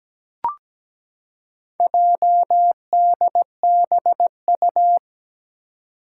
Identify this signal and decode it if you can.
Morse code, heard as a keyed tone, about 17 wpm, 700 Hz. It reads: JDBU
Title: Four Challenging Letter-Number Combinations 17wpm